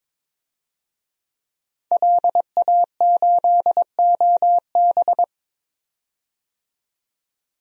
Morse code, heard as a keyed tone, about 22 wpm, 700 Hz. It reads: LA8OB
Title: DXCC-100 Call Signs - 5 Characters - 22wpm